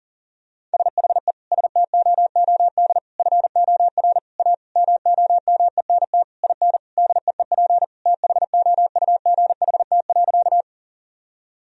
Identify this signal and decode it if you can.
Morse code, heard as a keyed tone, about 40 wpm, 700 Hz. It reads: SHE STOOD FOR A MOMENT IN DEEP THOUGHT.
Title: Sentences from Top 1000 Words - Review Part 3 - 40wpm